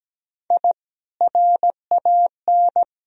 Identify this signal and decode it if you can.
Morse code, heard as a keyed tone, about 17 wpm, 700 Hz. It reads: I RAN
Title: Sentences from Top 1000 Words - Review Part 3 - 17wpm